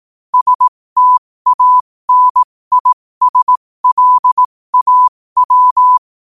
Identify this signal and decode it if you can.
Morse code, heard as a keyed tone, about 18 wpm, spaced as for 15 wpm, 1000 Hz. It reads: STANISLAW
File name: qotdfiles / 15wpm